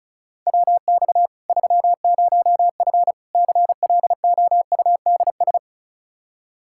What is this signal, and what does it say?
Morse code, heard as a keyed tone, about 36 wpm, 700 Hz. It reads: WX 30F CLOUDS
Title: Practice QSO - Exchange 2 - Info - 35wpm